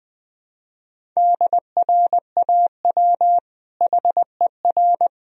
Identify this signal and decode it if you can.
Morse code, heard as a keyed tone, about 21 wpm, 700 Hz. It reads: DRAW HER
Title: Sentences from Top 1000 Words - Review Part 1 - 20wpm